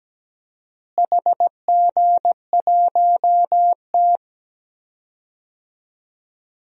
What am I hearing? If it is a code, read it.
Morse code, heard as a keyed tone, about 17 wpm, 700 Hz. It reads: HG1T
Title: DXCC-100 Call Signs - 4 Characters - Repeated 3x - 17wpm